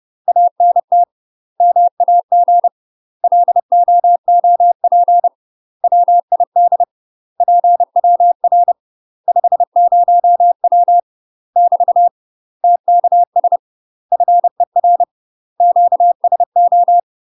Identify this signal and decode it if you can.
Morse code, heard as a keyed tone, about 30 wpm, 700 Hz. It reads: ANT MAG LOOP WID PWR 50W <BT> TKS FER QSO
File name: UK MASTER QSOs TRAINING/30WPM